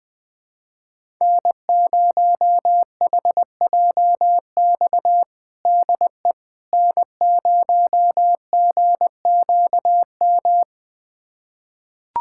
Morse code, heard as a keyed tone, about 21 wpm, 700 Hz. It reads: N0HJX DE N0GQM
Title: USA Callsign DE Callsign - 5 Characters - 20wpm